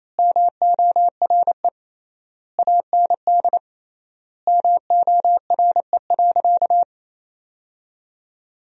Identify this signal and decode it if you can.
Morse code, heard as a keyed tone, about 28 wpm, 700 Hz. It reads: MORE AND MORE.
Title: 3x Word-Spacing - Sentences from Top 500 Words - 28wpm